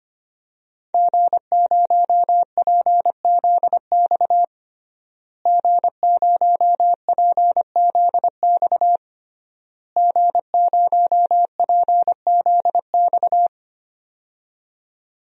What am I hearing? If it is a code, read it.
Morse code, heard as a keyed tone, about 25 wpm, 700 Hz. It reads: G0PZX G0PZX G0PZX
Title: DXCC-100 Call Signs - 5 Characters - Repeated 3x - 25wpm